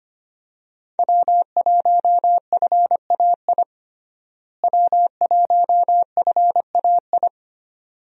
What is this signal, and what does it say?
Morse code, heard as a keyed tone, about 25 wpm, 700 Hz. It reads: W1FAI W1FAI
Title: USA Call Signs - 5 Characters - Repeated 2x - 25wpm